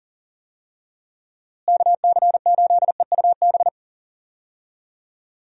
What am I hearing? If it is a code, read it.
Morse code, heard as a keyed tone, about 40 wpm, 700 Hz. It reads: KC8EUB
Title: USA Call Signs - 6 Characters - Repeated 2x - 40wpm